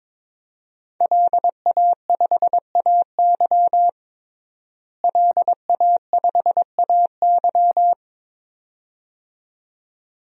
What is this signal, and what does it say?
Morse code, heard as a keyed tone, about 22 wpm, 700 Hz. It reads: LA5AY LA5AY
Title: DXCC-100 Call Signs - 5 Characters - Repeated 2x - 22wpm